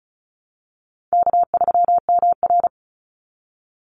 Morse code, heard as a keyed tone, about 36 wpm, 700 Hz. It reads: K3MR
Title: Call Signs - 35wpm